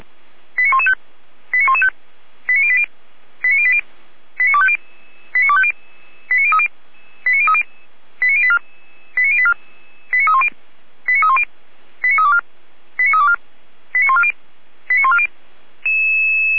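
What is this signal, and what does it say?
Five-tone selective call (ZVEI), heard as quick runs of five tones, tones of about 70 ms. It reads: 89115 89115 89080 89080 89250 89250 89030 89030 89084 89084 89310 89310 89324 89324 89140 89140
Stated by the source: Five-tone ZVEI selective call received by SDR, from Feuerwehr Orlamünde audio archive.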